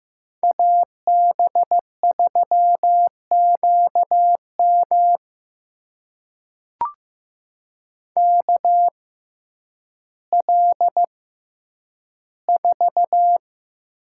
Morse code, heard as a keyed tone, about 15 wpm, spaced as for 9 wpm, 700 Hz. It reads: AB3QM KL4
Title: USA Call Signs - 6x Character Spacing - 15wpm